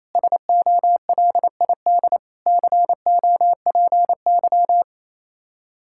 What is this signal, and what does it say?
Morse code, heard as a keyed tone, about 28 wpm, 700 Hz. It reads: SOLID COPY
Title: Practice QSO - Exchange 3 - More Info - 28wpm